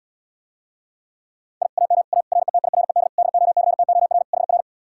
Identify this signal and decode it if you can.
Morse code, heard as a keyed tone, about 79 wpm, 700 Hz. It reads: I AM A RETIRED DOCTOR HR